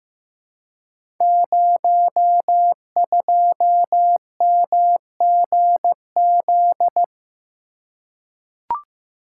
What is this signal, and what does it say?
Morse code, heard as a keyed tone, about 15 wpm, 700 Hz. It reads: 02MGZ
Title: Code Groups - 5 Letter:Numbers - 15wpm